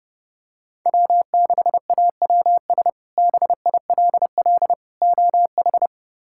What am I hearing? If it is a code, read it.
Morse code, heard as a keyed tone, about 30 wpm, 700 Hz. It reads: W6AWS BILL OH
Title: K1USN SST QSO Practice - 30wpm